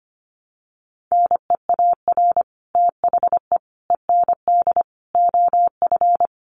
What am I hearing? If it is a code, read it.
Morse code, heard as a keyed tone, about 25 wpm, 700 Hz. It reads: NEAR THE END OF
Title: Sets of 5 Words Encore 25wpm